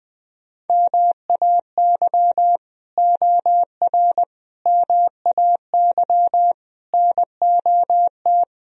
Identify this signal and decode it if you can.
Morse code, heard as a keyed tone, about 20 wpm, 700 Hz. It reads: MAY OR MAY NOT
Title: Speed-Racing - Sets of 5 Words 17wpm